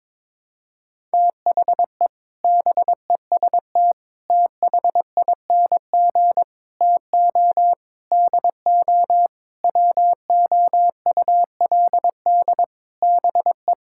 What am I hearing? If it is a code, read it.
Morse code, heard as a keyed tone, about 22 wpm, 700 Hz. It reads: THE BEST THING TO DO WOULD BE
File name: practice / 22wpm / Sentences with 8-Word Phrases - 22wpm